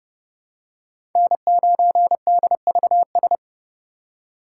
Morse code, heard as a keyed tone, about 32 wpm, 700 Hz. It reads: N9DVS
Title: USA Call Signs - 5 Characters - 2x Character Spacing - 30wpm